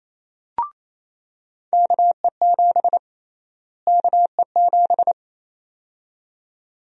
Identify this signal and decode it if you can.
Morse code, heard as a keyed tone, about 28 wpm, 700 Hz. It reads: KE7 KE7
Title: USA Call Sign Prefixes - Repeated 2x - 28wpm